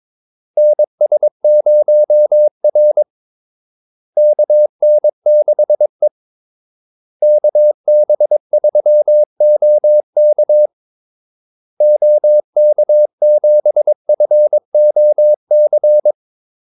Morse code, heard as a keyed tone, about 22 wpm, 600 Hz. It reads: NS0R KN6E KB3OK OK7FOC
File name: Callsigns-22wpm